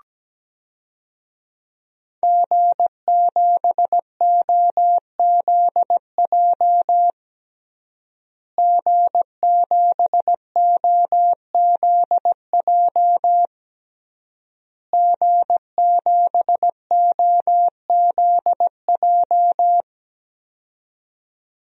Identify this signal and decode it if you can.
Morse code, heard as a keyed tone, about 17 wpm, 700 Hz. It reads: G7OZJ G7OZJ G7OZJ
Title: DXCC-100 Call Signs - 5 Characters - Repeated 3x - 17wpm